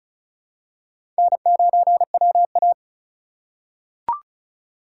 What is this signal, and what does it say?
Morse code, heard as a keyed tone, about 36 wpm, 700 Hz. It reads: N9WA
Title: USA Call Signs - 4x Character Spacing - 35wpm